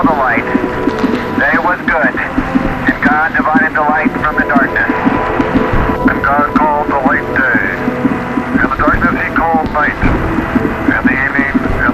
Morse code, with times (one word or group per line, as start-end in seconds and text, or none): none